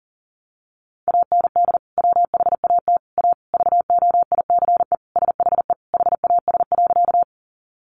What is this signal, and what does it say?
Morse code, heard as a keyed tone, about 40 wpm, 700 Hz. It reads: AND WHAT A VOICE SHE HAS.
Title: Sentences from Top 500 Words 40wpm